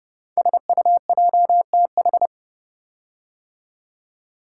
Morse code, heard as a keyed tone, about 30 wpm, 700 Hz. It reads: SUJTH
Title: Code Groups - 5 Letter:Numbers - 30wpm